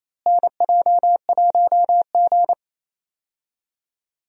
Morse code, heard as a keyed tone, about 28 wpm, 700 Hz. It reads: NJ1G
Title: USA Call Signs - 4 Characters - Repeated 2x - 28wpm